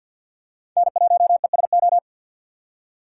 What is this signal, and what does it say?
Morse code, heard as a keyed tone, about 53 wpm, 700 Hz. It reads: N1ESO
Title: USA Call Signs - 5 Characters - 2x Character Spacing - 50wpm